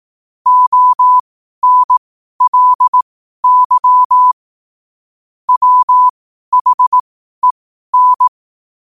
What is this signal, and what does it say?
Morse code, heard as a keyed tone, about 18 wpm, spaced as for 12 wpm, 1000 Hz. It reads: ONLY WHEN
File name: qotdfiles / 12wpm